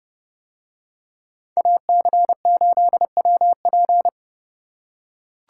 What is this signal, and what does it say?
Morse code, heard as a keyed tone, about 29 wpm, 700 Hz. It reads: AC8WP